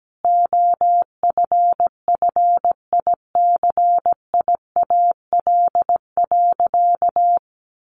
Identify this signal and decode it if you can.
Morse code, heard as a keyed tone, about 17 wpm, 700 Hz. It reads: OFFICIAL.
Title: Sentences from Top 1000 Words 17wpm